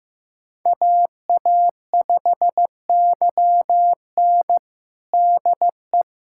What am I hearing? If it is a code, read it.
Morse code, heard as a keyed tone, about 15 wpm, 700 Hz. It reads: AA5YN DE
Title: USA Callsign DE Callsign - 5 Characters - 15wpm